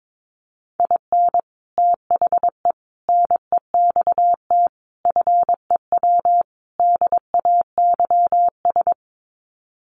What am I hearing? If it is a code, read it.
Morse code, heard as a keyed tone, about 22 wpm, 700 Hz. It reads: IN THE NEXT FEW DAYS
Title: Sets of 5 Words Encore2 22wpm